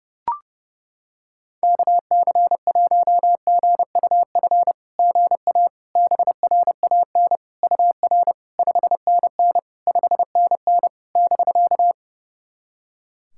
Morse code, heard as a keyed tone, about 31 wpm, 690 Hz. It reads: KC1GUF GA BRAN UR 5NN 5NN <BK>
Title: SOTA QSO Practice 30wpm